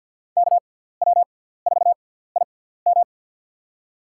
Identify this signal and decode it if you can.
Morse code, heard as a keyed tone, about 53 wpm, spaced as for 17 wpm, 700 Hz. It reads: KW4IM
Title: USA Call Signs - 5 Characters - 6x Character Spacing - 50wpm